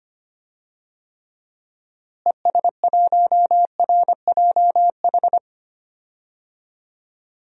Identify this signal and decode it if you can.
Morse code, heard as a keyed tone, about 25 wpm, 700 Hz. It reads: ES1RJH
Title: DXCC-100 Call Signs - 6 Characters - Repeated 2x - 25wpm